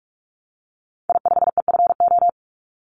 Morse code, heard as a keyed tone, about 44 wpm, 700 Hz. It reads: I5EFO
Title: Call Signs Repeated 3x - 45wpm